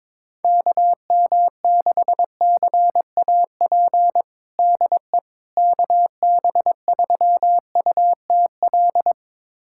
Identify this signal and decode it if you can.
Morse code, heard as a keyed tone, about 22 wpm, 700 Hz. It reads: KM6CAP DE KB3UTL
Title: USA Callsign DE Callsign 22wpm